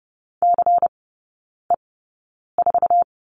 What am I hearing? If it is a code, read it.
Morse code, heard as a keyed tone, about 30 wpm, spaced as for 9 wpm, 700 Hz. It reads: CE4
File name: Lesson 16e - Letter c - Three Characters 30wpm